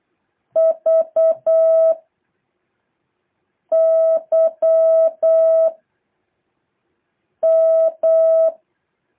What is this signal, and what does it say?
Morse code, heard as a keyed tone, about 8 wpm, 650 Hz. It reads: V Y M
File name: jh8cbh / 2023cw